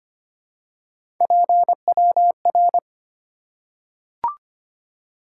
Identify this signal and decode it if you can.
Morse code, heard as a keyed tone, about 25 wpm, 700 Hz. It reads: PWR
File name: Top 90 QSO Elements - 8x Character Spacing - 25wpm